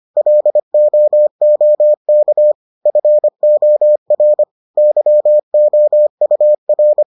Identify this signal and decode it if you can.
Morse code, heard as a keyed tone, about 25 wpm, 600 Hz. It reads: LOOK FOR YOUR